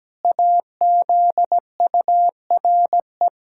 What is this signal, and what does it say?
Morse code, heard as a keyed tone, about 17 wpm, 700 Hz. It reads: AZURE
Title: Pangram Challenge v2 - 17wpm